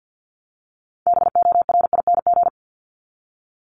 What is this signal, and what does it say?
Morse code, heard as a keyed tone, about 50 wpm, 700 Hz. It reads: BORING
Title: Top 1501-1750 Words 50wpm